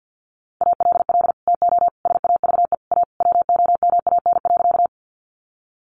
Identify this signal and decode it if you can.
Morse code, heard as a keyed tone, about 50 wpm, 700 Hz. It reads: ALL TO SAVE A WOMAN.
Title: Sentences from Top 1100 Words 50wpm